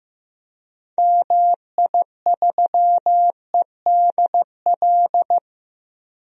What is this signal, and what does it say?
Morse code, heard as a keyed tone, about 15 wpm, 700 Hz. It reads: MI3EDL
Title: DXCC-100 Call Signs - 6 Characters - 15wpm